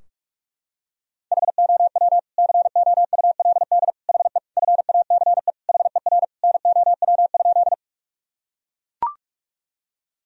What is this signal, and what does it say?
Morse code, heard as a keyed tone, about 42 wpm, 700 Hz. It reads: HOW COULD HE FACE HER NOW?
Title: Sentences from Top 1000 Words - Review Part 2 - 45wpm